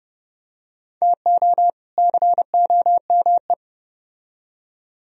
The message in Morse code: TO COME

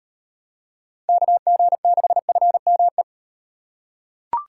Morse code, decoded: KG6FME

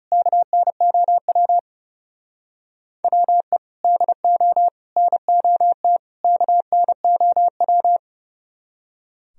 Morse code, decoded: KNOW WE DO NOT KNOW

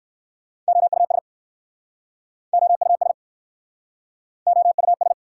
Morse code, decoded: OUS OUS OUS

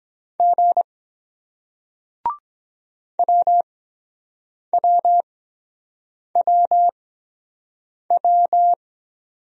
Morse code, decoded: G WWWW

